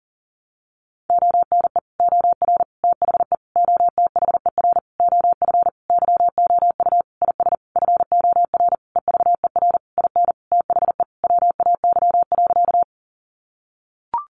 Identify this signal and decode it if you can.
Morse code, heard as a keyed tone, about 40 wpm, 700 Hz. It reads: ONE OR THE OTHER OF YOU IS FOR EVER IN THE WAY.
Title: Sentences from Top 400 Words 40wpm